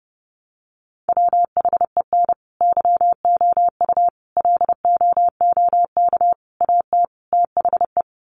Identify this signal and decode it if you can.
Morse code, heard as a keyed tone, about 30 wpm, 700 Hz. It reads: WHEN YOU LOOK AT THE
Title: Sets of 5 Words 30wpm